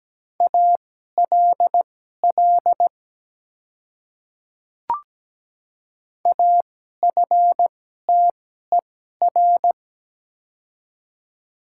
Morse code, heard as a keyed tone, about 17 wpm, spaced as for 12 wpm, 700 Hz. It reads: ALL AFTER
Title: 2x Character Spacing - Top 200 Words - 17wpm